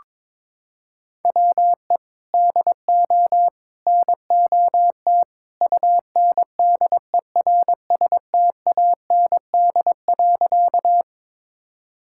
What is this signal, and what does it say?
Morse code, heard as a keyed tone, about 22 wpm, 700 Hz. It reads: WE DO NOT UNDERSTAND.
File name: practice / 22wpm / Sentences with 3-Word Phrases - 22wpm